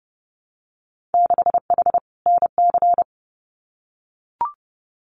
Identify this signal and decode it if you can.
Morse code, heard as a keyed tone, about 30 wpm, 700 Hz. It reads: 6H NC
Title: Winter Field Day Exchanges Encore 30wpm